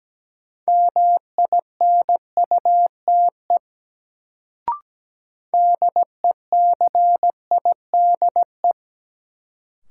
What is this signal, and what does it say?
Morse code, heard as a keyed tone, about 17 wpm, 700 Hz. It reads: MINUTE DECIDE
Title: Sentences from Top 500 Words - New Word Drill - 17wpm